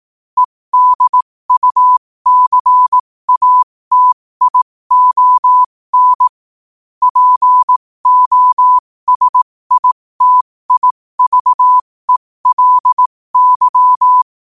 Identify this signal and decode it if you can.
Morse code, heard as a keyed tone, about 18 wpm, spaced as for 15 wpm, 1000 Hz. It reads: EDUCATION POSITIVELY
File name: qotdfiles / 15wpm